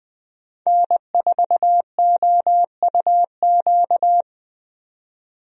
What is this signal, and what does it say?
Morse code, heard as a keyed tone, about 20 wpm, 700 Hz. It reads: N4OUQ
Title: USA Call Signs - 5 Characters - 20wpm